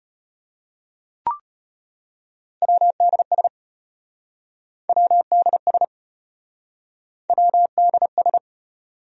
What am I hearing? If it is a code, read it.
Morse code, heard as a keyed tone, about 34 wpm, 700 Hz. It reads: WDS WDS WDS